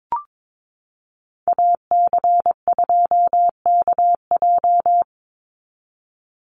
Morse code, heard as a keyed tone, about 22 wpm, 700 Hz. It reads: AC2KJ
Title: Call Signs Repeated 2x - 22wpm